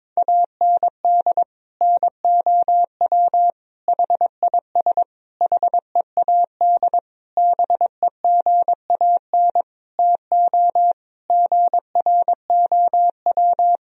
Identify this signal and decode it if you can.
Morse code, heard as a keyed tone, about 22 wpm, 700 Hz. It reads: AND NOW HIS HEAD BEGAN TO GROW
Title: Sentences from Top 1000 Words - Review Part 1 - 22wpm